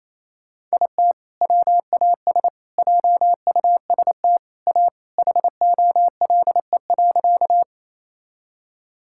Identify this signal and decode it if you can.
Morse code, heard as a keyed tone, about 28 wpm, 700 Hz. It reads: IT WAS JUST A HOLE.
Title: Sentences from Top 1600 Words 28wpm